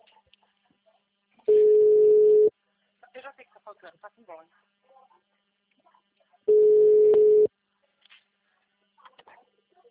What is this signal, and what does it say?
Ringback tone (Europe, 425 Hz), with speech.